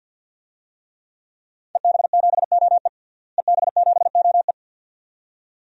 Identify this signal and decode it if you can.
Morse code, heard as a keyed tone, about 49 wpm, 700 Hz. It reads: EB7OE EB7OE
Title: DXCC-100 Call Signs - 5 Characters - Repeated 2x - 50wpm